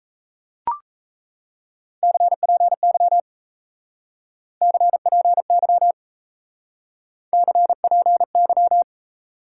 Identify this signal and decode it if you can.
Morse code, heard as a keyed tone, about 35 wpm, 700 Hz. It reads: CPY CPY CPY
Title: Speed-Racing Top 90 QSO Elements 28wpm